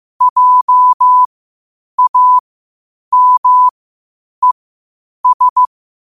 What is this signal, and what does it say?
Morse code, heard as a keyed tone, about 15 wpm, spaced as for 8 wpm, 1000 Hz. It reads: JAMES
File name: qotdfiles / 8wpm